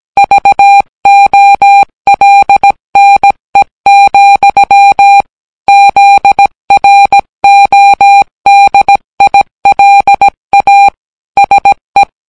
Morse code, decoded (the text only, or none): VOLNE, ZRODILA SE